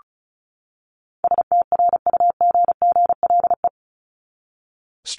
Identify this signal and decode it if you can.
Morse code, heard as a keyed tone, about 34 wpm, 700 Hz. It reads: STRUGGLE
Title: Top 1751-2000 Words 35wpm